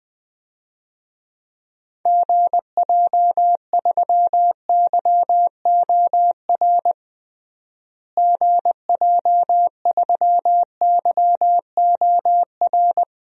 Morse code, heard as a keyed tone, about 20 wpm, 700 Hz. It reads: GJ3YOR GJ3YOR
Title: DXCC-100 Call Signs - 6 Characters - Repeated 3x - 20wpm